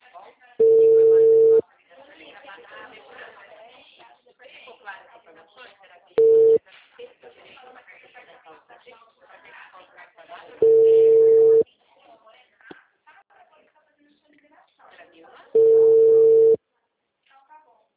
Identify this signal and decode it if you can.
Ringback tone (China, 450 Hz), with speech.